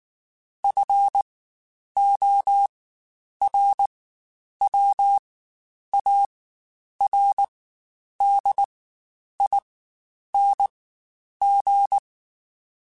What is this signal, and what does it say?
Morse code, heard as a keyed tone, about 19 wpm, spaced as for 9 wpm, 800 Hz. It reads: FORWARDING
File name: Spaced / Chuckles Spaced-mp3s / Temptation sp @18